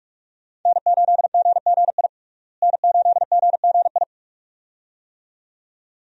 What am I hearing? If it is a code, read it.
Morse code, heard as a keyed tone, about 43 wpm, 700 Hz. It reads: N8GGI N8GGI
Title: USA Call Signs - 5 Characters - Repeated 2x - 45wpm